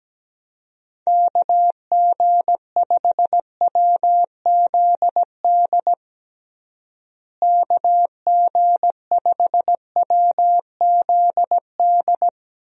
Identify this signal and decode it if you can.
Morse code, heard as a keyed tone, about 17 wpm, 700 Hz. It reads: KG5WZD KG5WZD
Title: USA Call Signs - 6 Characters - Repeated 2x - 17wpm